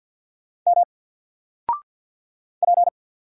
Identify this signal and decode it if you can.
Morse code, heard as a keyed tone, about 50 wpm, 700 Hz. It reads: M P
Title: Single Character (Letter:Number:Punctuation:Prosign) 50wpm